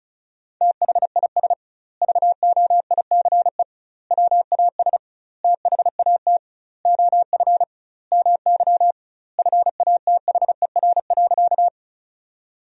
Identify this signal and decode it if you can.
Morse code, heard as a keyed tone, about 36 wpm, 700 Hz. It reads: THIS VOICE WAS THAT OF MY FATHER.
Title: 2x Word-Spacing - Sentences from Top 500 Words - 35wpm